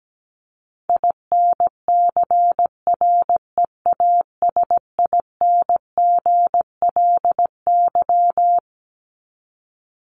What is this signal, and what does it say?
Morse code, heard as a keyed tone, about 17 wpm, 700 Hz. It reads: INCREASINGLY
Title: LY Suffix - Top 100 Words with Base Words 17wpm